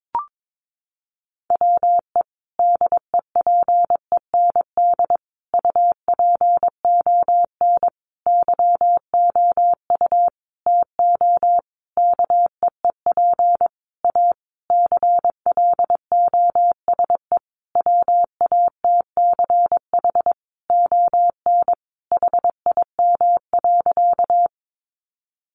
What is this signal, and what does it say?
Morse code, heard as a keyed tone, about 22 wpm, 700 Hz. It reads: WE DEPEND UPON YOU TO KEEP A CLOSE WATCH ON HIM.
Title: Sentences from Top 1300 Words 22wpm